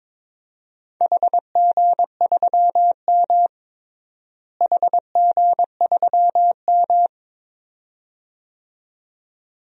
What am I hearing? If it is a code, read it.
Morse code, heard as a keyed tone, about 22 wpm, 700 Hz. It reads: HG3M HG3M
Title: DXCC-100 Call Signs - 4 Characters - Repeated 2x - 22wpm